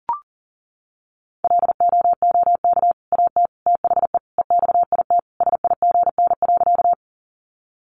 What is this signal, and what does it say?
Morse code, heard as a keyed tone, about 40 wpm, 700 Hz. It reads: LOOK AT THE EXIT SIGN.
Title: Sentences from Top 1100 Words 40wpm